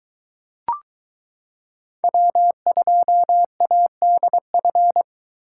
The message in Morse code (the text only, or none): W2ADF